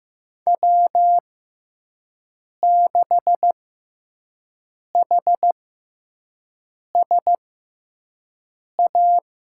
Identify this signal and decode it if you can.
Morse code, heard as a keyed tone, about 15 wpm, spaced as for 5 wpm, 700 Hz. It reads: W6HSA
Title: USA Call Signs - 5 Characters - 6x Character Spacing - 15wpm